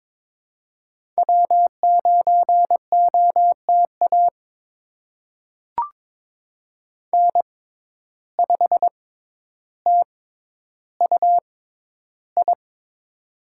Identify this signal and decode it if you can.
Morse code, heard as a keyed tone, about 22 wpm, spaced as for 11 wpm, 700 Hz. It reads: W9OTA N5TUI
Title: USA Call Signs - 5 Characters - 6x Character Spacing - 22wpm